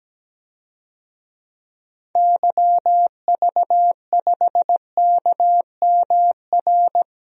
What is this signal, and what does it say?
Morse code, heard as a keyed tone, about 17 wpm, 700 Hz. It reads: YV5KMR
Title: DXCC-100 Call Signs - 6 Characters - Repeated 3x - 17wpm